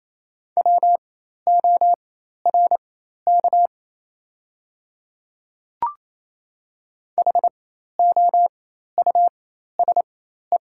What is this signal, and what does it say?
Morse code, heard as a keyed tone, about 28 wpm, spaced as for 13 wpm, 690 Hz. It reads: WORK HOUSE
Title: 4x Character Spacing - Top 200 Words - 28wpm